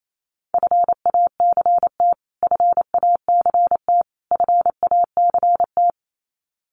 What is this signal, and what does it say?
Morse code, heard as a keyed tone, about 28 wpm, 700 Hz. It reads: FACT FACT FACT
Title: Top 500 Words Repeated 28wpm